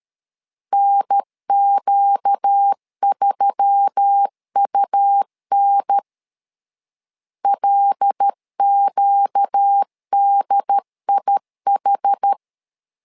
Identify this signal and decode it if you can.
Morse code, heard as a keyed tone, about 13 wpm, 800 Hz. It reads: NQ3UN LQDIH
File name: LEZIONI / 5 LBVC / 5 LBV 12wpm 1855